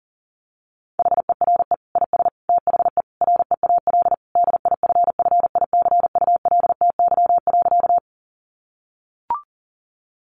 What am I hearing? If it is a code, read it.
Morse code, heard as a keyed tone, about 40 wpm, 700 Hz. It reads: HERE IS THE REAL DIFFICULTY.
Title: Sentences from Top 1500 Words 40wpm